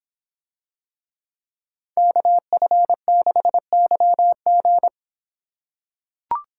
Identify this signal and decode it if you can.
Morse code, heard as a keyed tone, about 26 wpm, 700 Hz. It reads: KF6YG